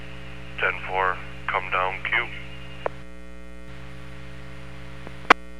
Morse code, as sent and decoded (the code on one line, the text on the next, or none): none